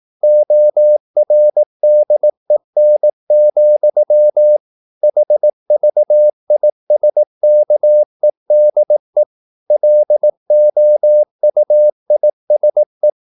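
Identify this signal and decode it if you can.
Morse code, heard as a keyed tone, about 18 wpm, 600 Hz. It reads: ORDEN, HVISKEDE LOUISE
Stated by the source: Slow (QRS) Morse practice at 18 wpm, CW med Gnister.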